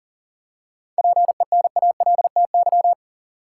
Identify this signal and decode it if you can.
Morse code, heard as a keyed tone, about 40 wpm, 700 Hz. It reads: PENALTY
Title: Seven Letter Words - 40wpm